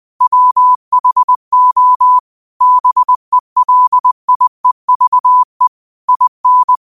Morse code, heard as a keyed tone, about 20 wpm, 1000 Hz. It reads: WHO BELIEVE IN